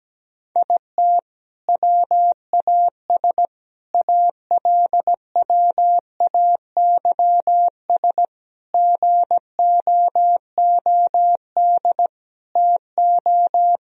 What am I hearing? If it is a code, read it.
Morse code, heard as a keyed tone, about 17 wpm, 700 Hz. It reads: IT WAS ALWAYS GOOD TO
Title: Sentences from Top 1000 Words - Review Part 4 - 17wpm